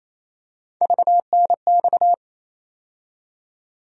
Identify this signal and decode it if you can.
Morse code, heard as a keyed tone, about 28 wpm, 700 Hz. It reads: VNX
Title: USA Call Sign Postfixes - 3 Letters 28wpm